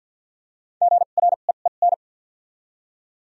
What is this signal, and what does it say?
Morse code, heard as a keyed tone, about 52 wpm, spaced as for 37 wpm, 700 Hz. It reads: GREEN
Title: 2x Character Spacing - Top 500 Words - 50wpm